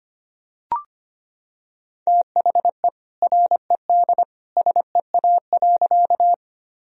Morse code, heard as a keyed tone, about 25 wpm, 700 Hz. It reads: THE RED SEA.
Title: Sentences from Top 1000 Words - Review Part 2 - 25wpm